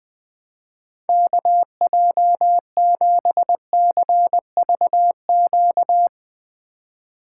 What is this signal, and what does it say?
Morse code, heard as a keyed tone, about 20 wpm, 700 Hz. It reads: KJ7CVQ
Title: Speed Racing - USA 2x3 Call Signs 20wpm